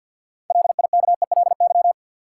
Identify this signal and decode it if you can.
Morse code, heard as a keyed tone, about 53 wpm, 700 Hz. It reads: LIKELY
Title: Sentences from Top 600 Words - New Word Drill - 50wpm